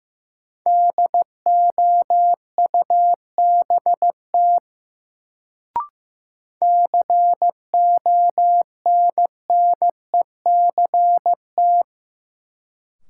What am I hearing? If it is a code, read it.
Morse code, heard as a keyed tone, about 15 wpm, 700 Hz. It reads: DOUBT CONNECT
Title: Sentences from Top 1500 Words - New Word Drill - 15wpm